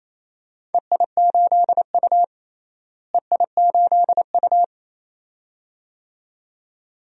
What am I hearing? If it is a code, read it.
Morse code, heard as a keyed tone, about 27 wpm, 700 Hz. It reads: EI8U EI8U
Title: DXCC-100 Call Signs - 4 Characters - Repeated 2x - 28wpm